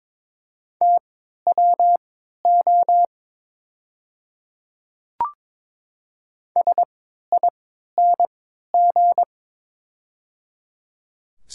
Morse code, heard as a keyed tone, about 22 wpm, spaced as for 12 wpm, 700 Hz. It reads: TWO SING
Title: 3x Character Spacing - Top 500 Words - 22wpm